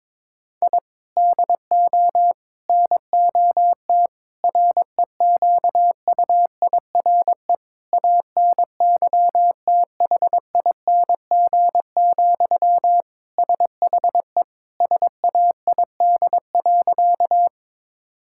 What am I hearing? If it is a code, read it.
Morse code, heard as a keyed tone, about 22 wpm, 700 Hz. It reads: I DO NOT REQUIRE ANYTHING, SHE SAID.
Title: Sentences from Top 1000 Words - Review Part 6 - 22wpm